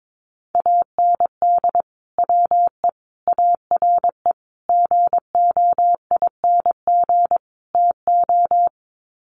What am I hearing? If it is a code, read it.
Morse code, heard as a keyed tone, about 22 wpm, 700 Hz. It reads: AND WE ARE GOING TO
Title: Sets of 5 Words Encore2 22wpm